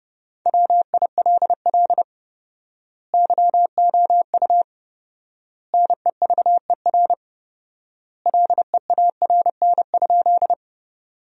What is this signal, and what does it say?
Morse code, heard as a keyed tone, about 30 wpm, 700 Hz. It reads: WILL YOU NEVER LEARN?
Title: 4x Word-Spacing - Sentences from Top 300 Words - 30wpm